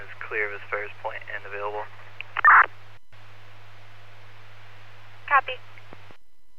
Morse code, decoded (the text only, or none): none